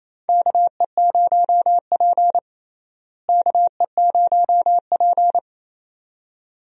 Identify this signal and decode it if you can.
Morse code, heard as a keyed tone, about 28 wpm, 700 Hz. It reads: KE0P KE0P
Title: USA Call Signs - 4 Characters - Repeated 2x - 28wpm